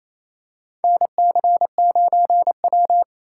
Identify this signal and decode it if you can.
Morse code, heard as a keyed tone, about 28 wpm, 700 Hz. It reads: NC9W
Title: K1USN SST QSO Practice - 28wpm